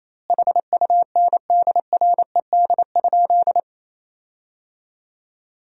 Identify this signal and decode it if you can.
Morse code, heard as a keyed tone, about 28 wpm, 700 Hz. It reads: HUNDRED?
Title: 2x Word-Spacing - Sentences from Top 400 Words - 28wpm